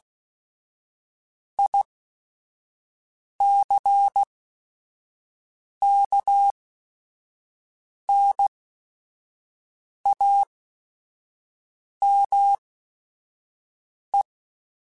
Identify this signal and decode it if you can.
Morse code, heard as a keyed tone, about 16 wpm, spaced as for 5 wpm, 800 Hz. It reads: ICKNAME
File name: Technical Subjects / Technical Subjects-mp3s / AND Gate @5